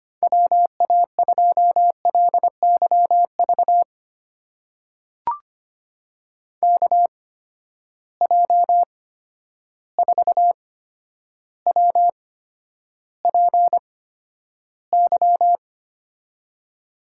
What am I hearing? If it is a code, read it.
Morse code, heard as a keyed tone, about 25 wpm, spaced as for 11 wpm, 700 Hz. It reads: WA2LYV KJ4WPY
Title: USA Call Signs - 8x Character Spacing - 25wpm